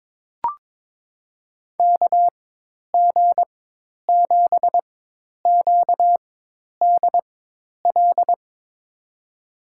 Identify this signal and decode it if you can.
Morse code, heard as a keyed tone, about 21 wpm, spaced as for 10 wpm, 700 Hz. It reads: KG7QDL
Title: USA Call Signs - 6 Characters - 4x Character Spacing - 22wpm